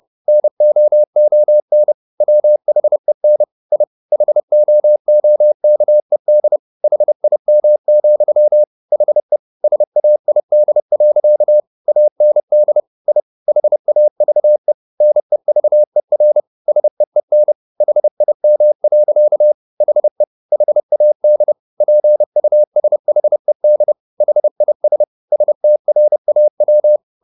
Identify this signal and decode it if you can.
Morse code, heard as a keyed tone, about 30 wpm, 600 Hz. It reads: NOON WHEN I HOOKED HIM, HE SAID. AND I HAVE NEVER SEEN HIM. HE HAD PUSHED HIS STRAW
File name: the old man and the sea / the old man and the sea 1 30wpm